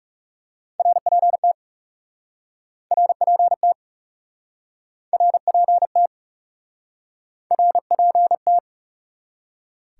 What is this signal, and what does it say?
Morse code, heard as a keyed tone, about 40 wpm, 700 Hz. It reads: RPT RPT RPT RPT